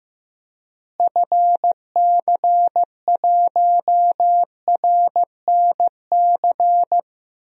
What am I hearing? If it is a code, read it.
Morse code, heard as a keyed tone, about 15 wpm, 700 Hz. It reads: FC1RNC
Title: DXCC-100 Call Signs - 6 Characters - Repeated 3x - 15wpm